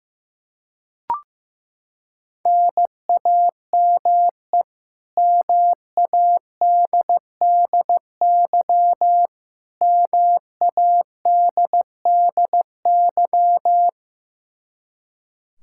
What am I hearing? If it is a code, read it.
Morse code, heard as a keyed tone, about 15 wpm, 700 Hz. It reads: NAME MADDY MADDY
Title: Practice QSO - Exchange 1 - Beginning - 15wpm